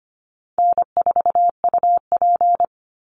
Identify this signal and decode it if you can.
Morse code, heard as a keyed tone, about 25 wpm, 700 Hz. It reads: N4UP
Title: Easy Call Signs - 25wpm